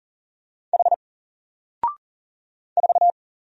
Morse code, heard as a keyed tone, about 40 wpm, 700 Hz. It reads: H 4
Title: Single Character (Letter:Number:Punctuation:Prosign) Alt 40wpm